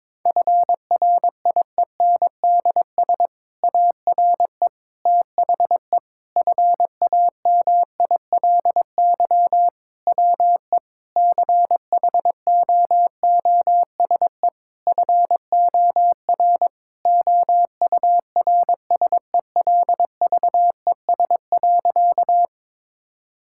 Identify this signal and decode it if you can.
Morse code, heard as a keyed tone, about 22 wpm, 700 Hz. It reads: FRIENDS ARE THE FAMILY WE CHOOSE FOR OURSELVES.
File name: Difficult Sentences with Hints - 22wpm